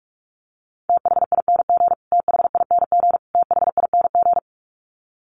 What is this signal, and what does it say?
Morse code, heard as a keyed tone, about 44 wpm, 700 Hz. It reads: THING THING THING
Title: Top 500 Words Repeated 45wpm